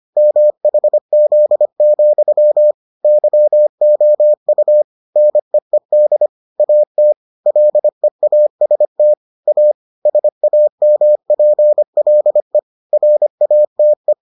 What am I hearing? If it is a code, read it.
Morse code, heard as a keyed tone, about 25 wpm, 600 Hz. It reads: MHZ, YOU NEED AT LEAST A SAMPLE RATE